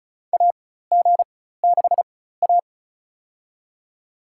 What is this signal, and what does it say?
Morse code, heard as a keyed tone, about 36 wpm, spaced as for 16 wpm, 700 Hz. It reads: AG6A
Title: USA Call Signs - 4 Characters - 4x Character Spacing - 35wpm